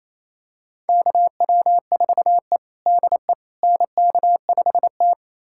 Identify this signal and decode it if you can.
Morse code, heard as a keyed tone, about 28 wpm, 700 Hz. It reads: KW4E DE NK5T
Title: USA Callsign DE Callsign - 4 Characters - 28wpm